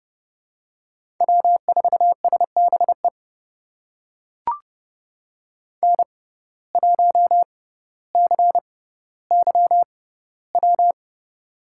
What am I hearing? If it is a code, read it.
Morse code, heard as a keyed tone, about 30 wpm, spaced as for 15 wpm, 700 Hz. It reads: W4SBE N1CYW
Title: USA Call Signs - 5 Characters - 6x Character Spacing - 30wpm